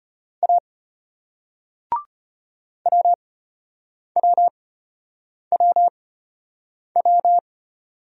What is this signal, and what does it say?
Morse code, heard as a keyed tone, about 33 wpm, spaced as for 8 wpm, 700 Hz. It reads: A WWWW